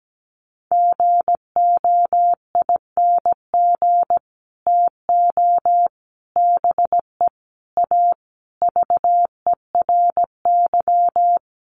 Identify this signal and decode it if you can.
Morse code, heard as a keyed tone, about 17 wpm, 700 Hz. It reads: GOING TO BE A VERY
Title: Sets of 5 Words Encore 17wpm